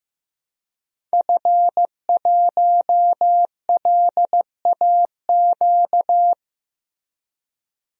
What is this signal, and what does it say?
Morse code, heard as a keyed tone, about 15 wpm, 700 Hz. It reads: F1LAQ
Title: DXCC-100 Call Signs - 5 Characters - Repeated 2x - 15wpm